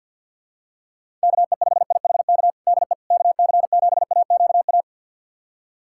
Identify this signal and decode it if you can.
Morse code, heard as a keyed tone, about 52 wpm, 700 Hz. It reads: KE5IHK DE KC7AQA
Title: USA Callsign DE Callsign - 6 Characters - 50wpm